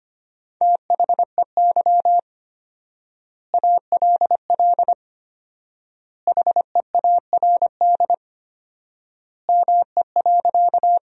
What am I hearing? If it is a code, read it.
Morse code, heard as a keyed tone, about 25 wpm, 700 Hz. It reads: THEY ALL HEARD ME.
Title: 4x Word-Spacing - Sentences from Top 400 Words - 25wpm